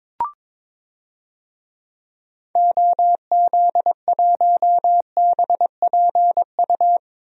Morse code, heard as a keyed tone, about 22 wpm, 700 Hz. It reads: OZ1BPU
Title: DXCC-100 Call Signs - 6 Characters - 22wpm